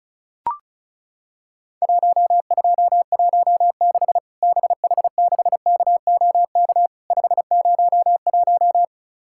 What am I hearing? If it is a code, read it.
Morse code, heard as a keyed tone, about 36 wpm, 700 Hz. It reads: 1216 BH6KOK 501
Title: Worked All Europe DX Contest - QTC Exchange - 35wpm